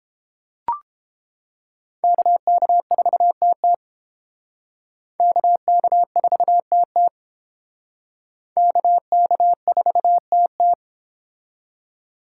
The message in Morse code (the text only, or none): KK4TT KK4TT KK4TT